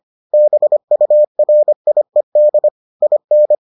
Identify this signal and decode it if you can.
Morse code, heard as a keyed tone, about 25 wpm, 600 Hz. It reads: BURIED IN